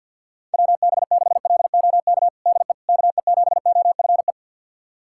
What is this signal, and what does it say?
Morse code, heard as a keyed tone, about 51 wpm, 700 Hz. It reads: WB6LOK DE KE7OFE
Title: USA Callsign DE Callsign - 6 Characters - 50wpm